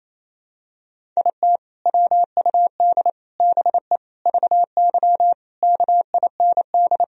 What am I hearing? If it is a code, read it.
Morse code, heard as a keyed tone, about 28 wpm, 700 Hz. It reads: IT WUD BE VY KIND
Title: Sentences from Top 200 Words and Abbreviations 28wpm